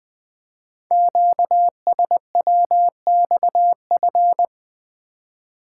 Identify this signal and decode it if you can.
Morse code, heard as a keyed tone, about 20 wpm, 700 Hz. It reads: QSWXF
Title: Code Groups - 5 Letters - 20wpm